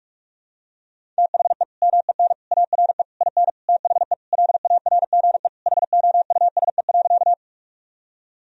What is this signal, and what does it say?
Morse code, heard as a keyed tone, about 41 wpm, 700 Hz. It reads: THE MEN ARE IN THE LARGE HOUSE.
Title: Sentences from Top 1000 Words - Review Part 1 - 45wpm